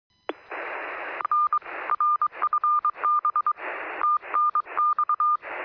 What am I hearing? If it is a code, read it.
Morse code, heard as a keyed tone, about 22 wpm, 1220 Hz. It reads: N RFB TNX
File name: sm7lcb / 0531z es0s 23cm